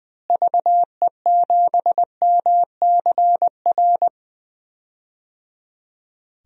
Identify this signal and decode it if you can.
Morse code, heard as a keyed tone, about 21 wpm, 700 Hz. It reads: VE7MCR
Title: DXCC-100 Call Signs - 6 Characters - Repeated 3x - 20wpm